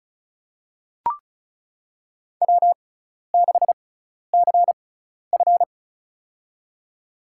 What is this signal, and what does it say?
Morse code, heard as a keyed tone, about 36 wpm, spaced as for 12 wpm, 700 Hz. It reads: W6CF
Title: USA Call Signs - 4 Characters - 6x Character Spacing - 35wpm